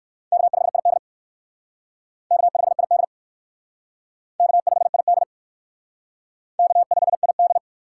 Morse code, heard as a keyed tone, about 55 wpm, 700 Hz. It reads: K5ID K5ID K5ID K5ID
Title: Speed Racing - USA Call Signs 45wpm